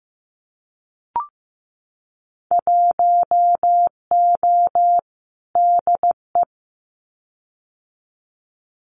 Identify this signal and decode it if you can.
Morse code, heard as a keyed tone, about 15 wpm, 700 Hz. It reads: 1O DE
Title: Winter Field Day Exchanges 15wpm